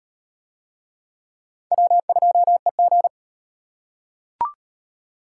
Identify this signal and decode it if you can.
Morse code, heard as a keyed tone, about 37 wpm, 700 Hz. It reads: W2EG